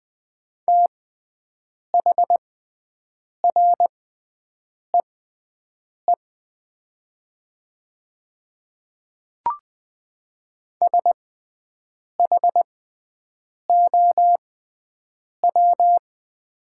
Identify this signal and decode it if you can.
Morse code, heard as a keyed tone, about 20 wpm, spaced as for 7 wpm, 700 Hz. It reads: THREE SHOW
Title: 6x Character Spacing - Top 500 Words - 20wpm